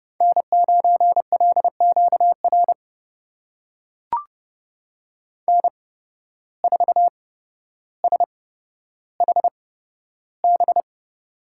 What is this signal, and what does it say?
Morse code, heard as a keyed tone, about 30 wpm, spaced as for 13 wpm, 700 Hz. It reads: N9LQR N4SHB
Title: USA Call Signs - 8x Character Spacing - 30wpm